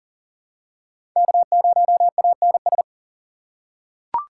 Morse code, heard as a keyed tone, about 40 wpm, 700 Hz. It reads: K0ANS
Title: USA Call Signs - 5 Characters - 4x Character Spacing - 40wpm